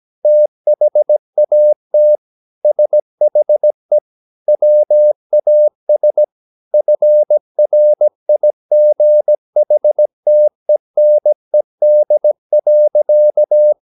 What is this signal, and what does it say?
Morse code, heard as a keyed tone, about 17 wpm, 600 Hz. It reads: THAT SHE WAS FRIGHTENED.
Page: Romanen Never af Ken Follet på engelsk.